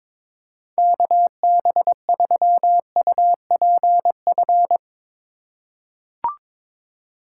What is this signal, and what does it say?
Morse code, heard as a keyed tone, about 22 wpm, 700 Hz. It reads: KB3UPF
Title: USA Call Signs - 6 Characters - 4x Character Spacing - 22wpm